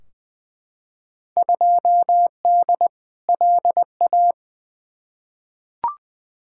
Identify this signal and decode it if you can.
Morse code, heard as a keyed tone, about 20 wpm, 700 Hz. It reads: 2D LA
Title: Field Day Exchanges - 20wpm